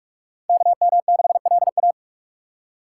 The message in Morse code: KM6LA

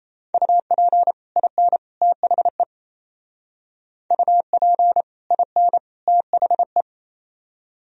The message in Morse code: UP IN THE UP IN THE

